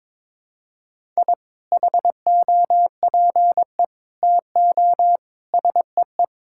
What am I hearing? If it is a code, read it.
Morse code, heard as a keyed tone, about 21 wpm, 700 Hz. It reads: I HOPE TO SEE
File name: practice / 22wpm / Sentences with 2-Word Phrases - 22wpm